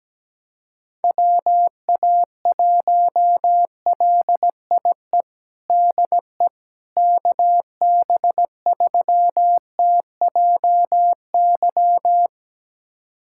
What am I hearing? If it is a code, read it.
Morse code, heard as a keyed tone, about 17 wpm, 700 Hz. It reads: WA1LIE DE KB3TJY
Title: USA Callsign DE Callsign - 6 Characters - 17wpm